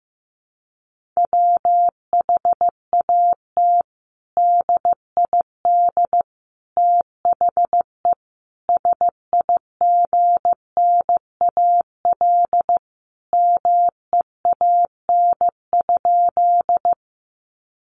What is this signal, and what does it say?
Morse code, heard as a keyed tone, about 15 wpm, 700 Hz. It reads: WHAT DID THE SIGNAL MEAN?
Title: Sentences from Top 1400 Words 15wpm